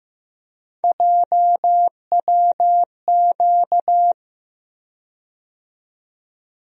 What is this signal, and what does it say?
Morse code, heard as a keyed tone, about 15 wpm, 700 Hz. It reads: JWQ
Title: USA Call Sign Postfixes - 3 Letters 15wpm